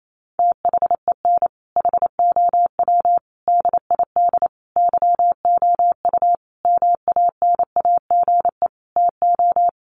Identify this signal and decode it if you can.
Morse code, heard as a keyed tone, about 28 wpm, 700 Hz. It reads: THEN HOW DID YOU MANAGE TO
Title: Sentences from Top 1400 Words 28wpm